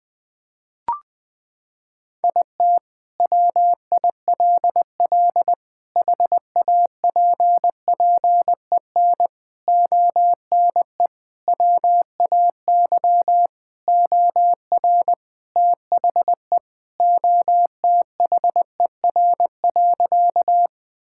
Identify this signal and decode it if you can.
Morse code, heard as a keyed tone, about 20 wpm, 700 Hz. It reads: IT WILL HAPPEN ONE WAY OR THE OTHER.
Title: Sentences with 5-Word Phrases - 20wpm